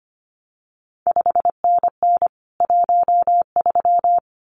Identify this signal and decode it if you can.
Morse code, heard as a keyed tone, about 25 wpm, 700 Hz. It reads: 5NN 13
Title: CQ WW DX Contest Exchange Encore - 25wpm